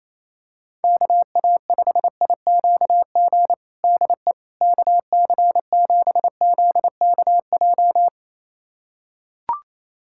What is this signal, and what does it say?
Morse code, heard as a keyed tone, about 28 wpm, 700 Hz. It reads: KA5IQG DE KC7ZKJ
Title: USA Callsign DE Callsign - 6 Characters - 28wpm